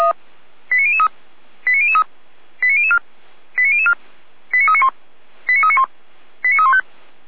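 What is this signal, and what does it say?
Five-tone selective call (ZVEI), heard as quick runs of five tones, tones of about 70 ms. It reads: 89003 89003 89004 89004 89391 89391 89315
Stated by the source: ZVEI five-tone selective call, code 89099, received with an SDR; fire brigade Feuerwehr Orlamünde.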